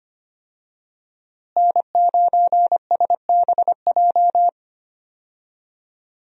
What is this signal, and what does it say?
Morse code, heard as a keyed tone, about 25 wpm, 700 Hz. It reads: N9SBJ
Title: USA Call Signs - 5 Characters - Repeated 3x - 25wpm